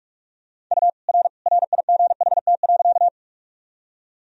Morse code, heard as a keyed tone, about 42 wpm, 700 Hz. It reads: U R RIGHT.
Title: Sentences from Top 200 Words and Abbreviations 45wpm